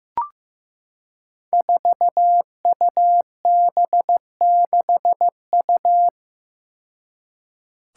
Morse code, heard as a keyed tone, about 15 wpm, 700 Hz. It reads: 4UB6U
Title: Five Challenging Letter-Number Combinations 15wpm